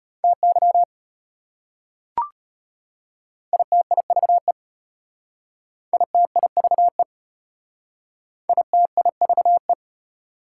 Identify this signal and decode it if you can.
Morse code, heard as a keyed tone, about 33 wpm, 700 Hz. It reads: TY ITIVE ITIVE ITIVE